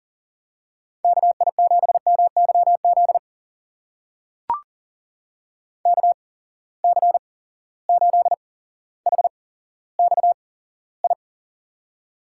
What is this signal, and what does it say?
Morse code, heard as a keyed tone, about 42 wpm, spaced as for 17 wpm, 700 Hz. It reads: KI7MYZ KC8HXI
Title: USA Call Signs - 6 Characters - 8x Character Spacing - 40wpm